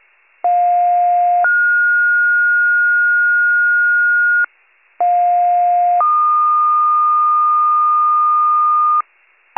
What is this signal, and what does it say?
Two-tone page, heard as a one-second tone, then a three-second tone, about 710 then 1430 Hz, about 710 then 1220 Hz, with noise. It is heard from 0.4 to 9.0 s.